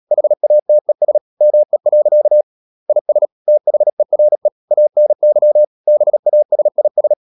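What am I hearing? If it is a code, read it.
Morse code, heard as a keyed tone, about 37 wpm, 600 Hz. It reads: HATES ME. IS THERE ANY BASIS
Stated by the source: Morse code practice at 35–39 wpm sending English text of Ken Follett's novel Never, from CW med Gnister.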